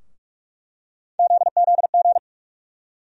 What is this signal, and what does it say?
Morse code, heard as a keyed tone, about 46 wpm, 700 Hz. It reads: ZZG
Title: USA Call Sign Postfixes - 3 Letters 45wpm